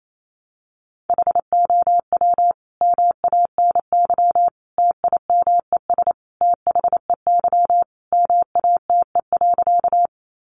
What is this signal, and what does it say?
Morse code, heard as a keyed tone, about 28 wpm, 700 Hz. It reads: HOW MANY TIMES THEY MATE.
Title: A Sound Of Thunder 28wpm